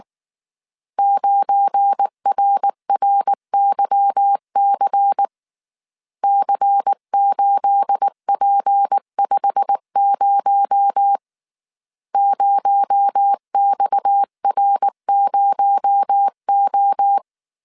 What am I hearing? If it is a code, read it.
Morse code, heard as a keyed tone, about 19 wpm, 800 Hz. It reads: 9RRYC C8P50 0XR0O
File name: CW / DET / 1 9 1500C S A 18wpm 2459